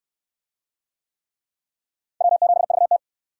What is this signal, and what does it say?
Morse code, heard as a keyed tone, about 68 wpm, 700 Hz. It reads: W7VT